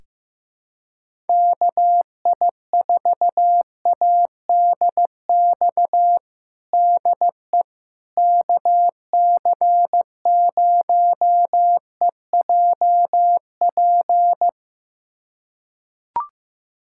Morse code, heard as a keyed tone, about 16 wpm, 700 Hz. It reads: KI4ADX DE KC0EJP
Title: USA Callsign DE Callsign - 6 Characters - 15wpm